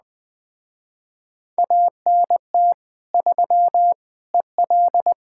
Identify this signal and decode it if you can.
Morse code, heard as a keyed tone, about 20 wpm, 700 Hz. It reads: ANT 3 EL
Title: Speed-Racing Common QSO Phrases - 17wpm